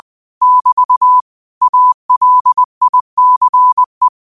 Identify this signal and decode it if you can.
Morse code, heard as a keyed tone, about 20 wpm, 1000 Hz. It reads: <BT> ALICE